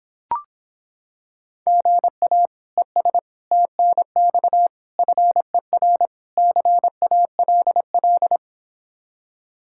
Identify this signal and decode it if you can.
Morse code, heard as a keyed tone, about 26 wpm, 700 Hz. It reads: GA ES TNX FER CALL